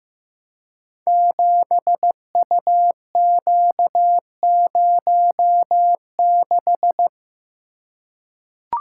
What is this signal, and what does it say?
Morse code, heard as a keyed tone, about 15 wpm, 700 Hz. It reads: 7UQ06
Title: Code Groups - 5 Letter:Numbers - 15wpm